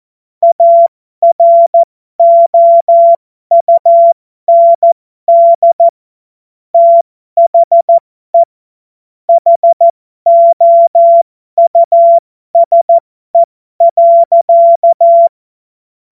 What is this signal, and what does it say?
Morse code, heard as a keyed tone, about 14 wpm, spaced as for 12 wpm, 680 Hz. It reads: AROUND THE HOUSE.
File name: Morsekurs / Witze